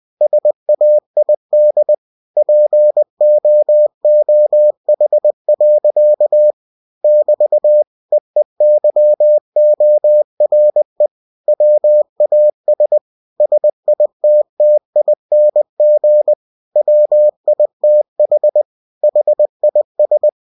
Morse code, heard as a keyed tone, about 20 wpm, 600 Hz. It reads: SAID POOH. <BT> EEYORE WAS SITTING WITH HIS
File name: cw-books / winnie-the-pooh / winnie-20wpm-page-0087